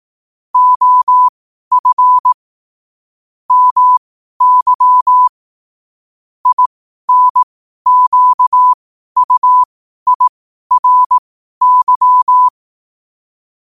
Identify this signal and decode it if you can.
Morse code, heard as a keyed tone, about 18 wpm, spaced as for 12 wpm, 1000 Hz. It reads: OF MY INQUIRY
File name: qotdfiles / 12wpm